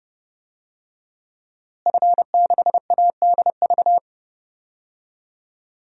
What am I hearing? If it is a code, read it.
Morse code, heard as a keyed tone, about 30 wpm, 700 Hz. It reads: F6ADV
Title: DXCC-100 Call Signs - 5 Characters - Repeated 3x - 30wpm